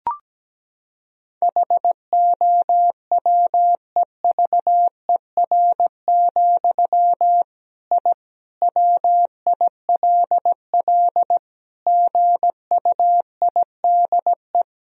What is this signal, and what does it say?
Morse code, heard as a keyed tone, about 17 wpm, 700 Hz. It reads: HOWEVER, I WILL GUIDE
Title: Sentences from Top 1000 Words - Review Part 3 - 17wpm